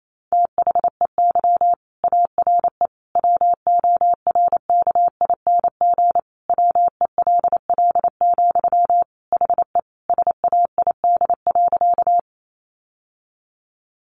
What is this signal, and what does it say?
Morse code, heard as a keyed tone, about 28 wpm, 700 Hz. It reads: THEY ARE WORKING WELL, HE SAID.
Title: Sentences from Top 700 Words 28wpm